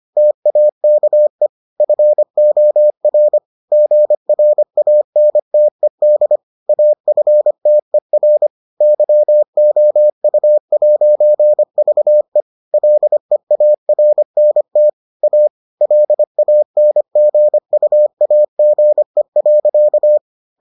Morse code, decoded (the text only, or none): TAKE FOR GRANTED AFTER YOU'VE LEARNT A LANGUAGE.